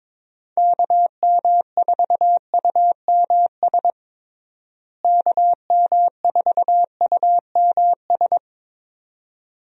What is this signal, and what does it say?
Morse code, heard as a keyed tone, about 22 wpm, 700 Hz. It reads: KM4UMS KM4UMS
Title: USA Call Signs - 6 Characters - Repeated 2x - 22wpm